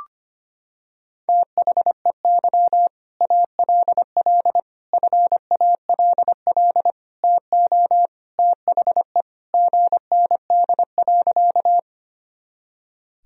Morse code, heard as a keyed tone, about 25 wpm, 700 Hz. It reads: THEY ALL FALL TO THE GND.
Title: Sentences from Top 500 Words and Abbreviations 25wpm